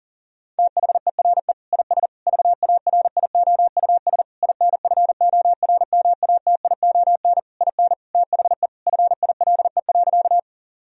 Morse code, heard as a keyed tone, about 40 wpm, 700 Hz. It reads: THERE IS VARIOUS INFORMATION IN THE FILE.
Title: Sentences from Top 1000 Words - Review Part 4 - 40wpm